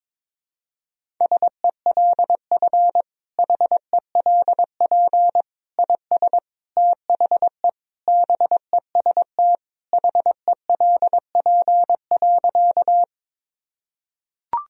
Morse code, heard as a keyed tone, about 22 wpm, 700 Hz. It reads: SELF HELP IS THE BEST HELP.
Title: Sentences from Top 1000 Words - Review Part 1 - 22wpm